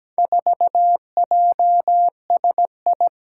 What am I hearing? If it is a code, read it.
Morse code, heard as a keyed tone, about 17 wpm, 700 Hz. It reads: 4JSI
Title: Four Challenging Letter-Number Combinations 17wpm